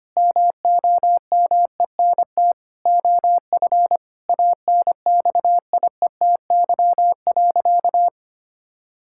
Morse code, heard as a keyed tone, about 25 wpm, 700 Hz. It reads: MOMENT OF ANXIETY.
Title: Sentences from Top 1600 Words 25wpm